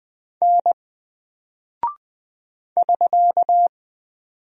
Morse code, heard as a keyed tone, about 20 wpm, 700 Hz. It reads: N <SK>
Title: Single Character (Letter:Number:Punctuation:Prosign) 20wpm